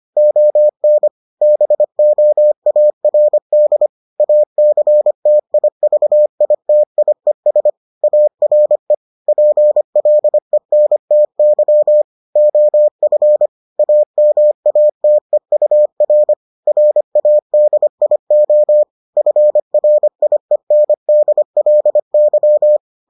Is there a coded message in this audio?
Yes: Morse code from 0.2 to 22.8 s, heard as a keyed tone, about 25 wpm, 600 Hz. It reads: ON BOARD ACTIVITIES ARE PLENTY OF AMATEUR RADIO FRIENDLY